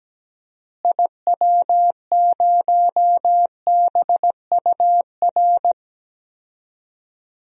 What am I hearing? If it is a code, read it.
Morse code, heard as a keyed tone, about 17 wpm, 700 Hz. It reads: IW0BUR
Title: DXCC-100 Call Signs - 6 Characters - Repeated 3x - 17wpm